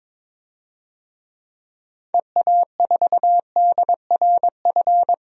Morse code, heard as a keyed tone, about 22 wpm, 700 Hz. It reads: EA4DRF
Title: DXCC-100 Call Signs - 6 Characters - 22wpm